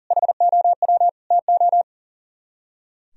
Morse code, heard as a keyed tone, about 51 wpm, 700 Hz. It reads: HOW TO